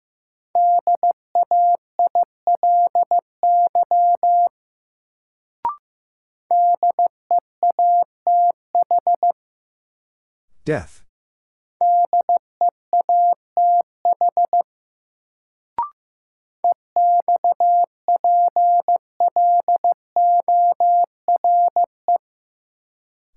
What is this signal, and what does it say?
Morse code, heard as a keyed tone, about 15 wpm, 700 Hz. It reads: DAILY DEATH DEATH EXPLORE
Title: Sentences from Top 1100 Words - New Word Drill - 15wpm